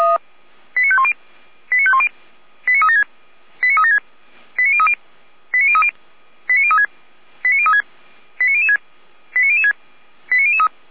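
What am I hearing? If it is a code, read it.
Five-tone selective call (ZVEI), heard as quick runs of five tones, tones of about 70 ms. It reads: 89510 89510 89376 89376 89030 89030 89036 89036 89006 89006 89003
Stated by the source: Five-tone ZVEI selective call 89011, received on SDR.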